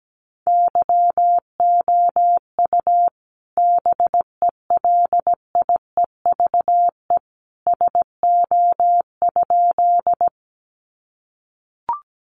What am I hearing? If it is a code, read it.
Morse code, heard as a keyed tone, about 17 wpm, 700 Hz. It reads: YOU BELIEVE SO?
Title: Sentences from Top 1000 Words 17wpm